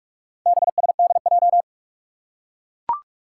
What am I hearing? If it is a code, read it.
Morse code, heard as a keyed tone, about 44 wpm, 700 Hz. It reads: BSDJ
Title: Four Challenging Letter-Number Combinations 45wpm